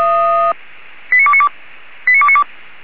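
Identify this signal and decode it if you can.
Five-tone selective call (ZVEI), heard as quick runs of five tones, tones of about 70 ms. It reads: 89292 89292
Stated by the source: ZVEI five-tone call received by SDR, on an archive of audio.f85360.feuerwehr-orlamuende.org (Feuerwehr Orlamünde fire brigade).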